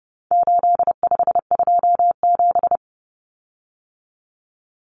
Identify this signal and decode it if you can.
Morse code, heard as a keyed tone, about 30 wpm, 700 Hz. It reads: 8527
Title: Common Cut Numbers - 4-Digit 30wpm